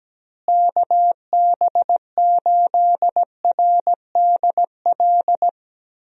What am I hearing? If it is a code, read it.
Morse code, heard as a keyed tone, about 17 wpm, 700 Hz. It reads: KB8RDL
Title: USA Call Signs - 6 Characters - Repeated 2x - 17wpm